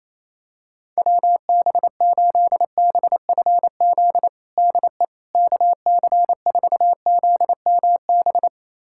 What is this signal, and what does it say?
Morse code, heard as a keyed tone, about 27 wpm, 700 Hz. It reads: WB8BFZ DE KC4ZMB
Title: USA Callsign DE Callsign - 6 Characters - 28wpm